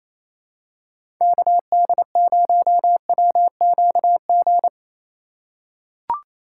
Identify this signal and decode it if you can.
Morse code, heard as a keyed tone, about 27 wpm, 700 Hz. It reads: KD0WQG
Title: USA Call Signs - 6x Character Spacing - 28wpm